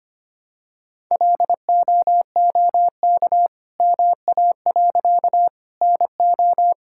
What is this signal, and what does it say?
Morse code, heard as a keyed tone, about 25 wpm, 700 Hz. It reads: LOOK MA. NO
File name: practice / 25wpm / All-Age Appropriate Jokes - 25wpm